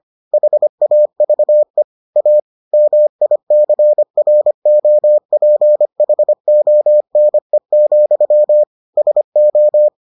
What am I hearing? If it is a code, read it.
Morse code, heard as a keyed tone, about 25 wpm, 600 Hz. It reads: HAVE A MICROPHONE, SO